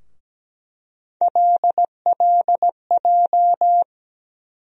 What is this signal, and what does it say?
Morse code, heard as a keyed tone, about 17 wpm, 700 Hz. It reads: LLJ
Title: USA Call Sign Postfixes - 3 Letters 17wpm